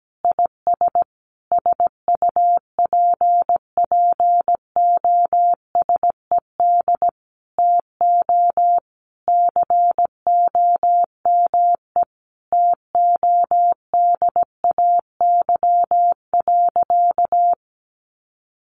Morse code, heard as a keyed tone, about 17 wpm, 700 Hz. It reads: IS SUPPOSED TO COME TODAY.
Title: Additional Easy Phrases 17wpm